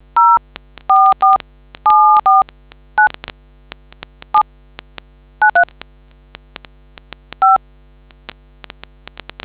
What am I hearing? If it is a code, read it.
DTMF, heard as telephone touch tones, tones of about 170 ms. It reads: 044749 793 5